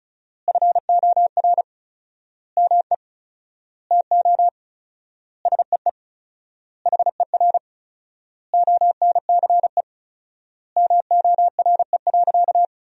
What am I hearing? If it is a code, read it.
Morse code, heard as a keyed tone, about 36 wpm, 700 Hz. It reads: FOR ME TO SEE HER ONCE MORE.
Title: 4x Word-Spacing - Sentences from Top 500 Words - 35wpm